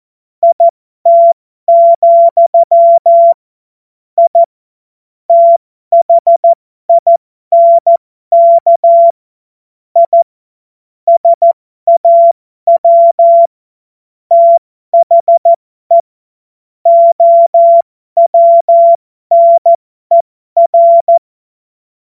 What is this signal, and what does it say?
Morse code, heard as a keyed tone, about 14 wpm, spaced as for 12 wpm, 680 Hz. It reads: IT, I THINK I SAW THE OWNER